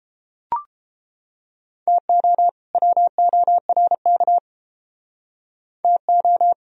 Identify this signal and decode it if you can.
Morse code, heard as a keyed tone, about 30 wpm, 700 Hz. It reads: TO WORK TO